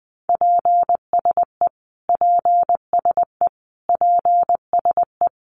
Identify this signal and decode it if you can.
Morse code, heard as a keyed tone, about 20 wpm, 700 Hz. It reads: PSE PSE PSE
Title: Abbreviations Repeated 3x - 20wpm